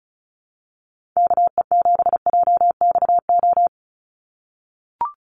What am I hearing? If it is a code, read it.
Morse code, heard as a keyed tone, about 34 wpm, 700 Hz. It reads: KE7JXO
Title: USA Call Signs 35wpm